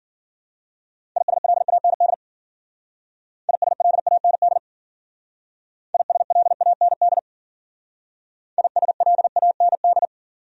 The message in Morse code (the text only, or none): ISLAND ISLAND ISLAND ISLAND